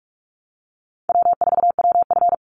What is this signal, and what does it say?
Morse code, heard as a keyed tone, about 44 wpm, 700 Hz. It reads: W4WF
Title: Easy Call Signs Repeated 2x - 45wpm